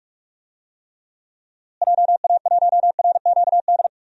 Morse code, heard as a keyed tone, about 46 wpm, 700 Hz. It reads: JA1RQD
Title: DXCC-100 Call Signs - 6 Characters - Repeated 3x - 45wpm